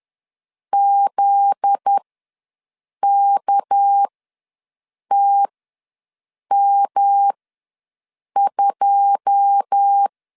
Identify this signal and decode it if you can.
Morse code, heard as a keyed tone, about 10 wpm, spaced as for 6 wpm, 800 Hz. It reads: ZKTM2
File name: LEZIONI / 10 SIGNS / 10C DOT ws 10wpm 4213